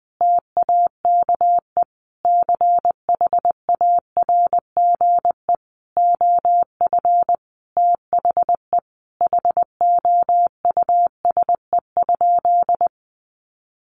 Morse code, heard as a keyed tone, about 19 wpm, 700 Hz. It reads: TAKE CHARGE OF THE HOUSE?
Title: Sentences from Top 900 Words 20wpm